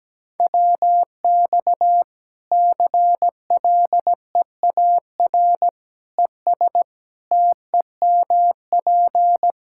Morse code, heard as a keyed tone, about 17 wpm, 700 Hz. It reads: WX CLEAR ES TEMP
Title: Practice QSO - Exchange 2 - Info - 17wpm